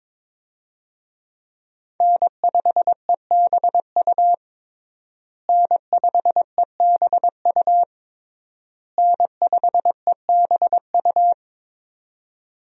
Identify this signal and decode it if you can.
Morse code, heard as a keyed tone, about 22 wpm, 700 Hz. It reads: N5EBU N5EBU N5EBU
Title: USA Call Signs - 5 Characters - Repeated 3x - 22wpm